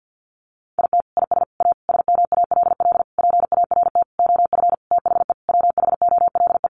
Morse code, heard as a keyed tone, about 50 wpm, 700 Hz. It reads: IT IS A SMALL PART OF THE WHOLE
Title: Sentences from Top 1500 Words 50wpm